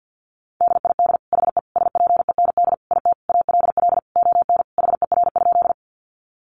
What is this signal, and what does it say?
Morse code, heard as a keyed tone, about 50 wpm, 700 Hz. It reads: DID HE SPEND IT ALL ON HER?
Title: Sentences from Top 800 Words 50wpm